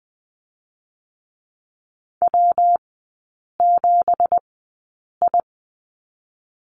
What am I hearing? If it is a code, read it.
Morse code, heard as a keyed tone, about 20 wpm, spaced as for 8 wpm, 690 Hz. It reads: W7I
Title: 172 Lesson 28e - Number 9 - Three Characters 20wpm